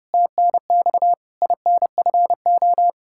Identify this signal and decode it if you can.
Morse code, heard as a keyed tone, about 30 wpm, 700 Hz. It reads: TNX INFO
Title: Practice QSO - Exchange 3 - More Info - 30wpm